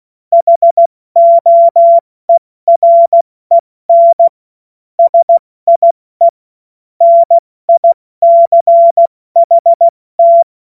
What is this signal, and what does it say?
Morse code, heard as a keyed tone, about 16 wpm, 680 Hz. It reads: HOEREN SIE NICHT
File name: Morsekurs / Witze / Witz 28. Dezember 2021, 70 F80